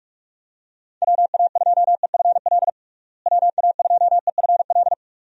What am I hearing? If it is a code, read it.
Morse code, heard as a keyed tone, about 46 wpm, 700 Hz. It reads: WA2EFL WA2EFL
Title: USA Call Signs - 6 Characters - Repeated 3x - 45wpm